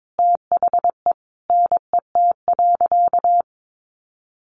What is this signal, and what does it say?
Morse code, heard as a keyed tone, about 22 wpm, 700 Hz. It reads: THE NET.
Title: Sentences from Top 1400 Words 22wpm